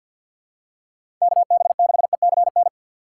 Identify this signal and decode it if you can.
Morse code, heard as a keyed tone, about 46 wpm, 690 Hz. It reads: KB6ECN
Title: USA Call Signs - 6 Characters - 3x Character Spacing - 50wpm